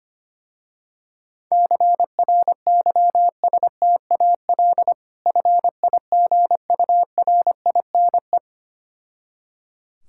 Morse code, heard as a keyed tone, about 25 wpm, 700 Hz. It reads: CRYSTAL FIGURINE
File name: practice / 25wpm / Difficult Words with Hints - 25wpm